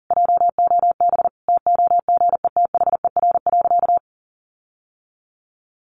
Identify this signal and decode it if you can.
Morse code, heard as a keyed tone, about 40 wpm, 700 Hz. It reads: JOB TOGETHER.
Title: Sentences from Top 1300 Words 40wpm